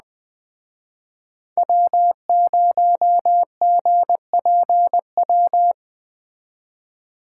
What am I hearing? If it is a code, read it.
Morse code, heard as a keyed tone, about 20 wpm, 700 Hz. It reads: W0GPW
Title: Speed Racing - USA 1x3 Call Signs 17wpm